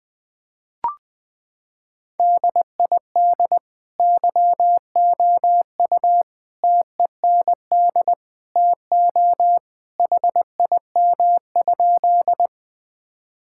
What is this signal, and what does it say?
Morse code, heard as a keyed tone, about 21 wpm, 700 Hz. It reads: DID YOU TEND TO HIM?
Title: Sentences from Top 1000 Words - Review Part 5 - 20wpm